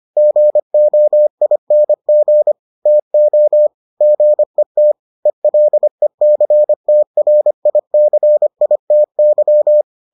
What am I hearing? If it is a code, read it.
Morse code, heard as a keyed tone, about 25 wpm, 600 Hz. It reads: GOING TO GET ELECTRICITY